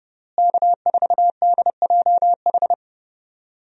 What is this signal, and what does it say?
Morse code, heard as a keyed tone, about 32 wpm, 700 Hz. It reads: K4DJH
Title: USA Call Signs - 5 Characters - Repeated 2x - 30wpm